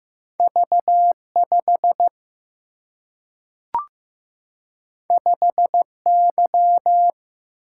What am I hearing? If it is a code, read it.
Morse code, heard as a keyed tone, about 15 wpm, 700 Hz. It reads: V5 5Y
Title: Single-Letter Single-Number 15wpm